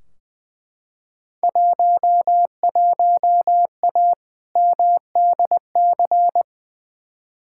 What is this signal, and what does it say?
Morse code, heard as a keyed tone, about 20 wpm, 700 Hz. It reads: 11A MDC
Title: Field Day Exchanges Encore - 20wpm